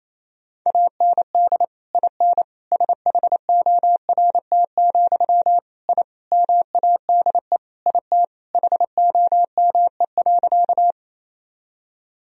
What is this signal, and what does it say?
Morse code, heard as a keyed tone, about 28 wpm, 700 Hz. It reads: AND IN SHORT, I MADE IT HOME.
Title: Sentences from Top 1000 Words - Review Part 2 - 28wpm